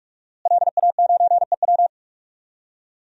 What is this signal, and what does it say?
Morse code, heard as a keyed tone, about 42 wpm, 700 Hz. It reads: LA9EW
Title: DXCC-100 Call Signs - 5 Characters - Repeated 2x - 45wpm